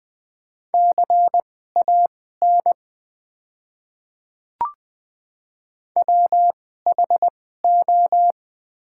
Morse code, heard as a keyed tone, about 20 wpm, spaced as for 14 wpm, 700 Hz. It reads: CAN WHO
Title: 2x Character Spacing - Top 100 Words - 20wpm